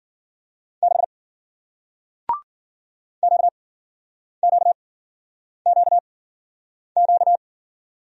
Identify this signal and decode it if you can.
Morse code, heard as a keyed tone, about 54 wpm, spaced as for 9 wpm, 700 Hz. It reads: 6 QQQQ